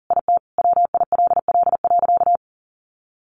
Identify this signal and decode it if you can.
Morse code, heard as a keyed tone, about 37 wpm, 700 Hz. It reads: IT WILL.
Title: Sentences from Top 1500 Words 40wpm